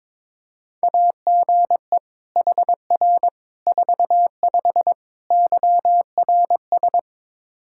Morse code, heard as a keyed tone, about 21 wpm, 700 Hz. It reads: AGE HR 45 YRS
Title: Practice QSO - Exchange 3 - More Info - 22wpm